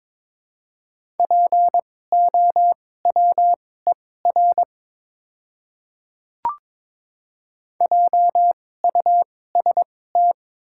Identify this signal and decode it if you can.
Morse code, heard as a keyed tone, about 22 wpm, spaced as for 16 wpm, 700 Hz. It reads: POWER JUST
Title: 2x Character Spacing - Top 500 Words - 22wpm